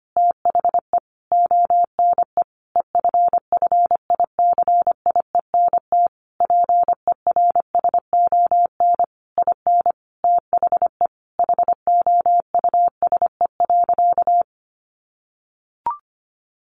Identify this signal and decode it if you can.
Morse code, heard as a keyed tone, about 25 wpm, 700 Hz. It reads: THE ONE EFFICIENT PERSON IN THE HOUSE.
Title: Sentences from Top 1300 Words 25wpm